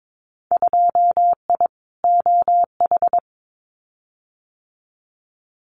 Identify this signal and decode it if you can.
Morse code, heard as a keyed tone, about 22 wpm, 700 Hz. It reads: 2I OH
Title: Winter Field Day Exchanges 22wpm